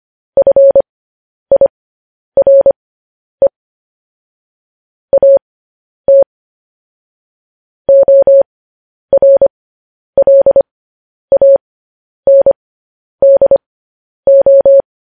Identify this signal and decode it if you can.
Morse code, heard as a keyed tone, about 25 wpm, spaced as for 10 wpm, 550 Hz. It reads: FIRE AT ORLANDO